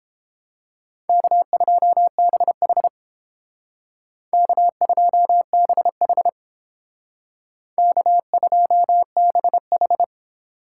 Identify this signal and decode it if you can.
Morse code, heard as a keyed tone, about 31 wpm, 700 Hz. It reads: K2BH K2BH K2BH